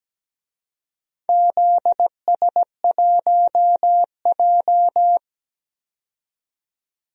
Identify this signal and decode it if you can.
Morse code, heard as a keyed tone, about 17 wpm, 700 Hz. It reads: ZS1J
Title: DXCC-100 Call Signs - 4 Characters - 17wpm